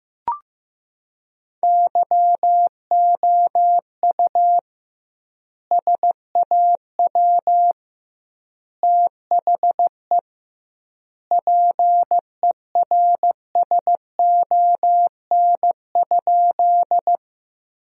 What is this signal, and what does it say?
Morse code, heard as a keyed tone, about 16 wpm, 700 Hz. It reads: YOU SAW THE PERSON?
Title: 2x Word-Spacing - Sentences from Top 400 Words - 15wpm